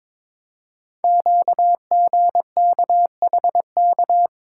Morse code, heard as a keyed tone, about 23 wpm, 700 Hz. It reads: QGKHK
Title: Code Groups - 5 Letters - 22wpm